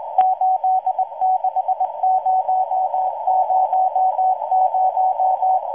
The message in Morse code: 86000